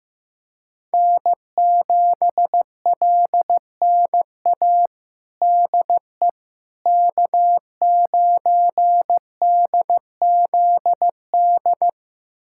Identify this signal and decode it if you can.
Morse code, heard as a keyed tone, about 15 wpm, 700 Hz. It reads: N7LNA DE K9DZD
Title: USA Callsign DE Callsign - 5 Characters - 15wpm